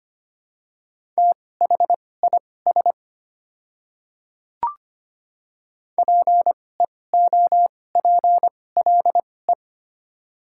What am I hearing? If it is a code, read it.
Morse code, heard as a keyed tone, about 25 wpm, spaced as for 18 wpm, 700 Hz. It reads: THIS PEOPLE
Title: 2x Character Spacing - Top 500 Words - 25wpm